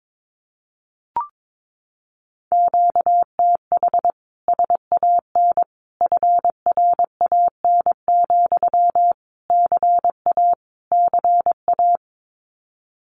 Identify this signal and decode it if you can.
Morse code, heard as a keyed tone, about 22 wpm, 700 Hz. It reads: QTH SAN FRAN, CA CA
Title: Common QSO Phrases Encore 22wpm